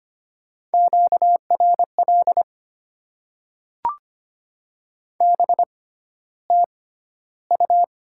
Morse code, heard as a keyed tone, about 25 wpm, spaced as for 13 wpm, 700 Hz. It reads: QRL BTU
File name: Top 20 QSO Elements - 6x Character Spacing - 25wpm